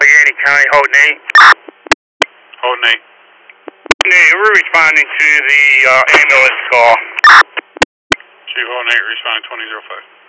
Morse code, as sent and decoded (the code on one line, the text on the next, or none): none